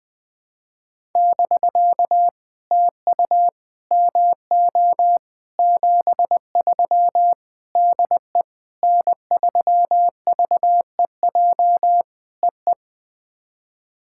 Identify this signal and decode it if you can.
Morse code, heard as a keyed tone, about 20 wpm, 700 Hz. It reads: <BK> TU MO 73 DE N3VEJ EE
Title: POTA QSO Practice - 20wpm